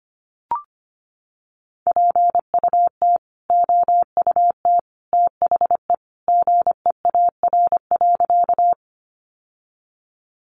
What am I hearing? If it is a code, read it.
Morse code, heard as a keyed tone, about 25 wpm, 700 Hz. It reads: PUT OUT THE GEAR.
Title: Sentences from Top 1400 Words 25wpm